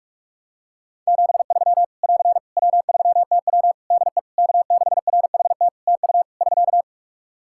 Morse code, heard as a keyed tone, about 46 wpm, 700 Hz. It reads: 73 <AR> W3TW DE K6RHT TU <SK>
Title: Practice QSO - Exchange 4 - Ending - 45wpm